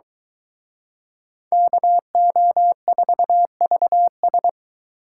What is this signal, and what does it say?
Morse code, heard as a keyed tone, about 23 wpm, 700 Hz. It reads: KO4VS